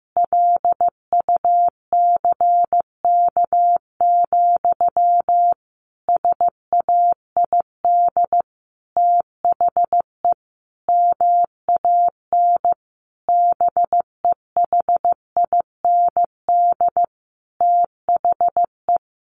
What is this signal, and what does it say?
Morse code, heard as a keyed tone, about 15 wpm, 700 Hz. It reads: LUCK, SAID THE MAN BEHIND THE
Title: A Sound Of Thunder 15wpm